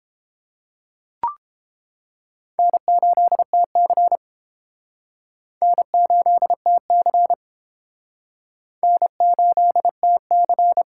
Morse code, heard as a keyed tone, about 28 wpm, 700 Hz. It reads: N8TC N8TC N8TC